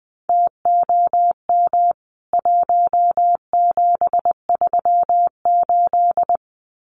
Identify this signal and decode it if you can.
Morse code, heard as a keyed tone, about 20 wpm, 700 Hz. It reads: TOM 1738
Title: CWT Exchange 20wpm